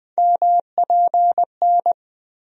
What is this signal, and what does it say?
Morse code, heard as a keyed tone, about 20 wpm, 700 Hz. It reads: MPN
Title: USA Call Sign Postfixes - 3 Letters 20wpm